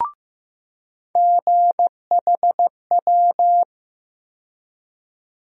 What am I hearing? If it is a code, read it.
Morse code, heard as a keyed tone, about 15 wpm, 700 Hz. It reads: GHW
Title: USA Call Sign Postfixes - 3 Letters 15wpm